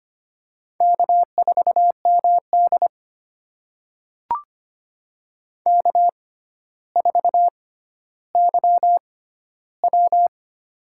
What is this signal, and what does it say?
Morse code, heard as a keyed tone, about 25 wpm, spaced as for 13 wpm, 700 Hz. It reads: K4MD K4YW
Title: USA Call Signs - 4 Characters - 6x Character Spacing - 25wpm